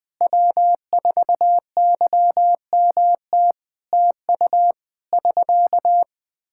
Morse code, heard as a keyed tone, about 20 wpm, 700 Hz. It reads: W4YMT TU <SK>
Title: Practice QSO - Exchange 4 - Ending - 20wpm